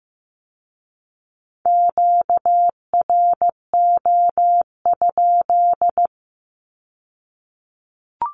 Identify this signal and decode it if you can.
Morse code, heard as a keyed tone, about 15 wpm, 700 Hz. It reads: QRO?
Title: Q-Codes with Spoken Meaning - 15wpm